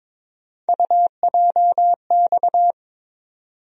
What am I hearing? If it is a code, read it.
Morse code, heard as a keyed tone, about 22 wpm, 700 Hz. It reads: UJX
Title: USA Call Sign Postfixes - 3 Letters 22wpm